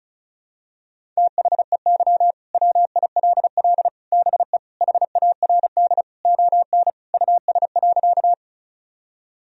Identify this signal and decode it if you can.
Morse code, heard as a keyed tone, about 34 wpm, 700 Hz. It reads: THEY WILL BE HARD ON US.
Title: Sentences from Top 1000 Words - Review Part 1 - 35wpm